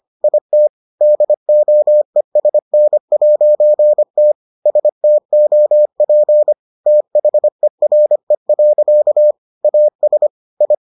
Morse code, heard as a keyed tone, about 25 wpm, 600 Hz. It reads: IT DOESN'T STOP THERE. AS I